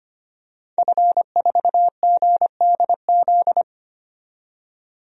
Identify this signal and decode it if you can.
Morse code, heard as a keyed tone, about 25 wpm, 700 Hz. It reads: F4GDZ
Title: DXCC-100 Call Signs - 5 Characters - 25wpm